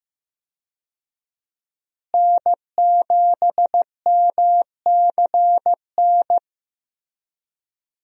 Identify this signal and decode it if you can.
Morse code, heard as a keyed tone, about 15 wpm, 700 Hz. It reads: N7MCN
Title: USA Call Signs - 5 Characters - Repeated 2x - 15wpm